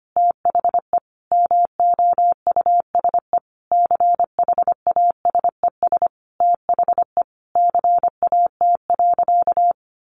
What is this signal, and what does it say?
Morse code, heard as a keyed tone, about 25 wpm, 700 Hz. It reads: THE MOUSE CHASES THE CAT.
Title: Easy Phrases 25wpm